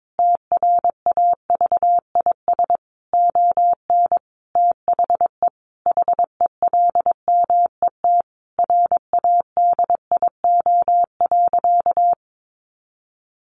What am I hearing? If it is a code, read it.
Morse code, heard as a keyed tone, about 22 wpm, 700 Hz. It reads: TRAVIS ON THE HELMET RADIO.
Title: A Sound Of Thunder 22wpm